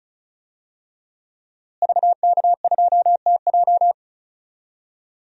Morse code, heard as a keyed tone, about 34 wpm, 700 Hz. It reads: VK2TJ
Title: DXCC-100 Call Signs - 5 Characters - Repeated 2x - 35wpm